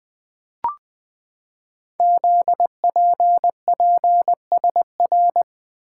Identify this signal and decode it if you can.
Morse code, heard as a keyed tone, about 20 wpm, 700 Hz. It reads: ZPPSR
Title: Code Groups - 5 Letters - 20wpm